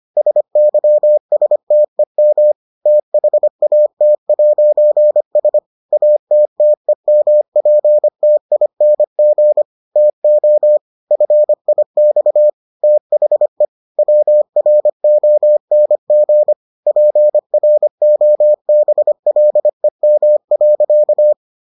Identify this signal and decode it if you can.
Morse code, heard as a keyed tone, about 25 wpm, 600 Hz. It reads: SYSTEM THAT'S ATTEMPTING TO FIX THE WRONG PROBLEM.